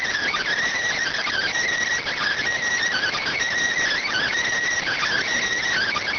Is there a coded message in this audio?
Yes: Emergency Alert System (SAME) tones (header bursts) from 1.0 to 5.3 s, heard as harsh buzzing bursts.